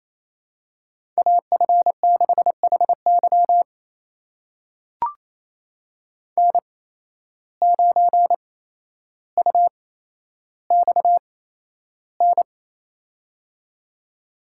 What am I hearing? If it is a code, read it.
Morse code, heard as a keyed tone, about 28 wpm, spaced as for 12 wpm, 700 Hz. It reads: AF6HY N9UXN
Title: USA Call Signs - 5 Characters - 8x Character Spacing - 28wpm